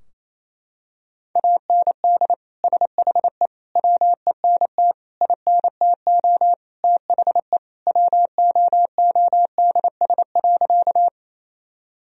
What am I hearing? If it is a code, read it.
Morse code, heard as a keyed tone, about 28 wpm, 700 Hz. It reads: AND SHE WENT INTO THE WOODS.
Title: Sentences from Top 1000 Words - Review Part 2 - 28wpm